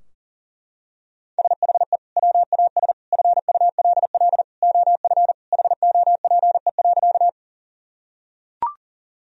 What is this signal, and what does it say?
Morse code, heard as a keyed tone, about 40 wpm, 700 Hz. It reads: SHE WAS FULL OF HOPE.
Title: Sentences from Top 1000 Words - Review Part 3 - 40wpm